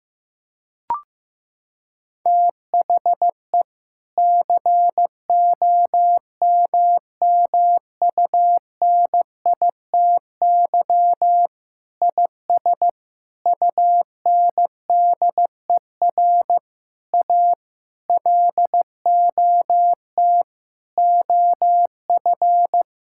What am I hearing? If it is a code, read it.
Morse code, heard as a keyed tone, about 15 wpm, 700 Hz. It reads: THE COMMUNITY IS UNDER A LOT OF
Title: Sentences from Top 1000 Words - Review Part 6 - 15wpm